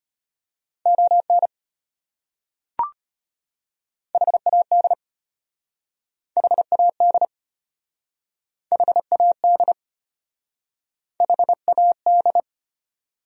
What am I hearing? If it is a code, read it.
Morse code, heard as a keyed tone, about 34 wpm, 700 Hz. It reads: ON HAD HAD HAD HAD